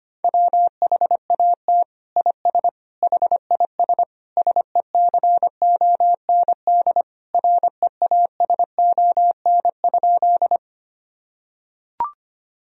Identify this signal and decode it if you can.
Morse code, heard as a keyed tone, about 25 wpm, 700 Hz. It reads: WHAT IS HIS SECOND REASON?
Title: Sentences from Top 1000 Words - Review Part 1 - 25wpm